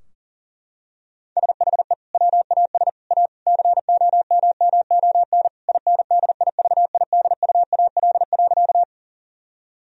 Morse code, heard as a keyed tone, about 37 wpm, 700 Hz. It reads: SHE WAS A COMMON INDIVIDUAL.
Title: Sentences from Top 1000 Words - Review Part 4 - 40wpm